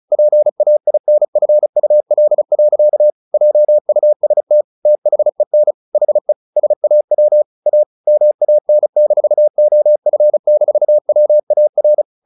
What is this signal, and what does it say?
Morse code, heard as a keyed tone, about 34 wpm, 600 Hz. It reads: PAINFUL. JUST THEN HE SAW A MAN-OF-WAR
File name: the old man and the sea / the old man and the sea 1 35wpm